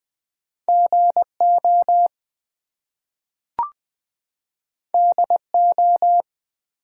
Morse code, heard as a keyed tone, about 20 wpm, 700 Hz. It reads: GO DO
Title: Two Letter Words - 20wpm